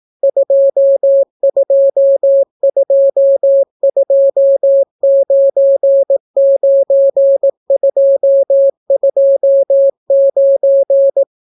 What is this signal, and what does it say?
Morse code, heard as a keyed tone, about 18 wpm, 550 Hz. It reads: 222299229